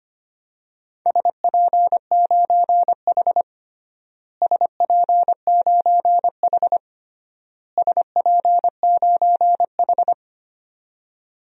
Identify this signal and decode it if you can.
Morse code, heard as a keyed tone, about 25 wpm, 700 Hz. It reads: SP9H SP9H SP9H
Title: DXCC-100 Call Signs - 4 Characters - Repeated 3x - 25wpm